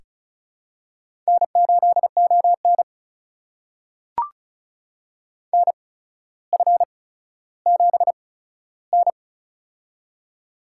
Morse code, heard as a keyed tone, about 34 wpm, spaced as for 15 wpm, 700 Hz. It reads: N8ON NF7N
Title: USA Call Signs - 4 Characters - 8x Character Spacing - 35wpm